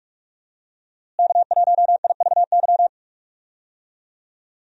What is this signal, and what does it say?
Morse code, heard as a keyed tone, about 46 wpm, 700 Hz. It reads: K1IVY
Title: USA Call Signs - 5 Characters - Repeated 2x - 45wpm